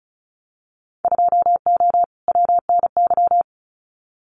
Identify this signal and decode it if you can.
Morse code, heard as a keyed tone, about 36 wpm, 700 Hz. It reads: 2O WNY
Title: Winter Field Day Exchanges Encore 35wpm